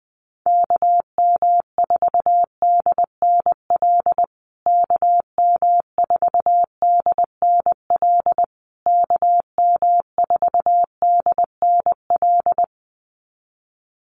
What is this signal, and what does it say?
Morse code, heard as a keyed tone, about 20 wpm, 700 Hz. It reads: KM4DNL KM4DNL KM4DNL
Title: USA Call Signs Repeated 3x 20wpm